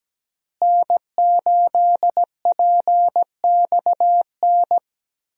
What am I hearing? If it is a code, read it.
Morse code, heard as a keyed tone, about 17 wpm, 700 Hz. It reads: N8PXN
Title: USA Call Signs - 5 Characters - Repeated 3x - 17wpm